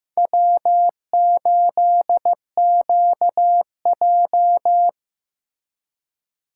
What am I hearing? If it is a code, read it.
Morse code, heard as a keyed tone, about 18 wpm, 700 Hz. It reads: W8QJ
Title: Speed Racing - USA 1x2 Call Signs 15wpm